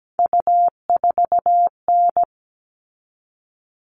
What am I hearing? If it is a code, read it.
Morse code, heard as a keyed tone, about 17 wpm, 700 Hz. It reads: U4N
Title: All Cut Numbers - 3-Digit 17wpm